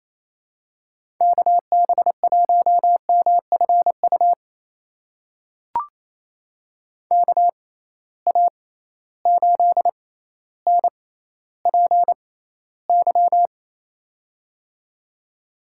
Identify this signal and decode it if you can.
Morse code, heard as a keyed tone, about 28 wpm, spaced as for 14 wpm, 700 Hz. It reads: KB1MFU KA8NPY
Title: USA Call Signs - 6 Characters - 6x Character Spacing - 28wpm